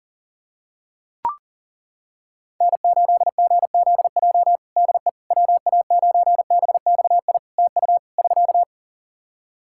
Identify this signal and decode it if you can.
Morse code, heard as a keyed tone, about 40 wpm, 700 Hz. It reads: N8GZJ DE WA9BXI TU <SK>
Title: K1USN SST QSO Practice 40wpm